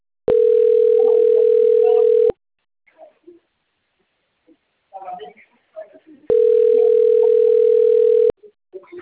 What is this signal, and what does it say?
Ringback tone (North America), 440 + 480 Hz, with speech.
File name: rg-1600-unknown- - - .54486